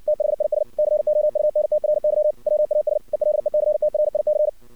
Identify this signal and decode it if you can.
Morse code, heard as a keyed tone, about 59 wpm, 610 Hz. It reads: THIS COUNTRY WAS ELECTRIC
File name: archived / CRSnet